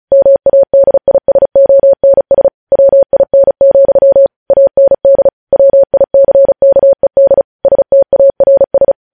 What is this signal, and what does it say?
Morse code, heard as a keyed tone, about 36 wpm, 550 Hz. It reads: MADISONS WIN, AND WICKED STARS
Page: News Headlines in Morse Code at 35 WPM